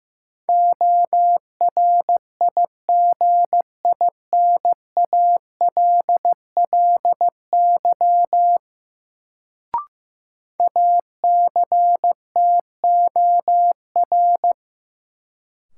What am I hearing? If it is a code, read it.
Morse code, heard as a keyed tone, about 15 wpm, 700 Hz. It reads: ORIGINALLY ACTOR
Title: Sentences from Top 1500 Words - New Word Drill - 15wpm